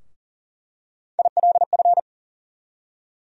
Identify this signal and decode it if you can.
Morse code, heard as a keyed tone, about 40 wpm, 700 Hz. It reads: ILF
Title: USA Call Sign Postfixes - 3 Letters 40wpm